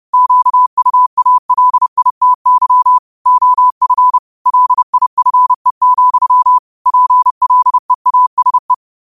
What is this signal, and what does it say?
Morse code, heard as a keyed tone, about 30 wpm, 1000 Hz. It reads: QUALITY OF LIFE, PLEASE
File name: qotdfiles / 30wpm